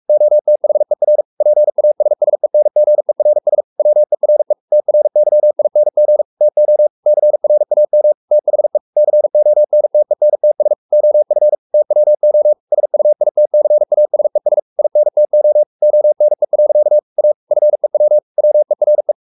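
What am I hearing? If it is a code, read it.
Morse code, heard as a keyed tone, about 44 wpm, 600 Hz. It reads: OTHER PASSENGERS WERE TRYING TO CRAM THE CONTENTS OF TWO SUITCASES INTO ONE. A FEW WERE
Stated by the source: Morse code at 43 wpm sending Ken Follett's novel Never in English, from CW med Gnister.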